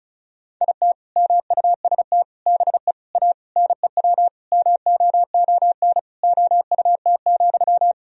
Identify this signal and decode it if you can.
Morse code, heard as a keyed tone, about 34 wpm, 700 Hz. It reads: IT MUST BE A NEW MOON OUT,
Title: Sentences from Top 500 Words and Abbreviations 35wpm